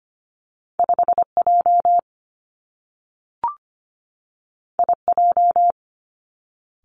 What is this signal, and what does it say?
Morse code, heard as a keyed tone, about 25 wpm, 700 Hz. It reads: 5 J I J
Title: Two Challenging Letter-Number Combinations 25wpm